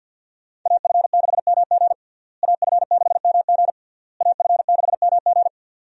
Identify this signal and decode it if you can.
Morse code, heard as a keyed tone, about 46 wpm, 700 Hz. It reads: AF6MG AF6MG AF6MG
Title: USA Call Signs - 5 Characters - Repeated 3x - 50wpm